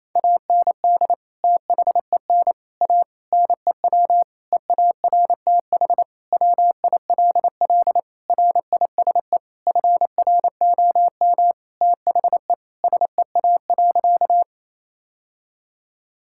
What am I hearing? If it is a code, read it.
Morse code, heard as a keyed tone, about 28 wpm, 700 Hz. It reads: AND THEN A NEW EARTH WILL RISE FROM THE SEA.
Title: Sentences from Top 1000 Words - Review Part 6 - 28wpm